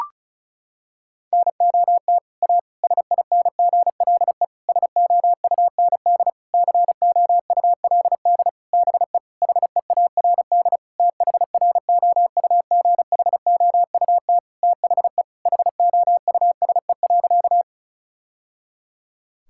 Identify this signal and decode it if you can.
Morse code, heard as a keyed tone, about 36 wpm, 700 Hz. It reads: NOT A SINGLE SOUND COULD BE HEARD THROUGHOUT THE HOUSE.
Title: Sentences from Top 1000 Words - Review Part 5 - 35wpm